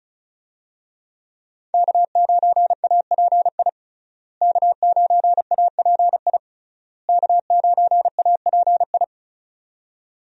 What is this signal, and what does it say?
Morse code, heard as a keyed tone, about 36 wpm, 700 Hz. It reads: K9API K9API K9API
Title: USA Call Signs - 5 Characters - Repeated 3x - 35wpm